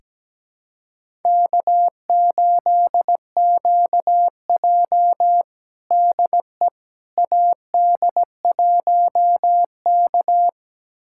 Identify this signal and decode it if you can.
Morse code, heard as a keyed tone, about 17 wpm, 700 Hz. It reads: K8QJ DE AD1K
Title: USA Callsign DE Callsign - 4 Characters - 17wpm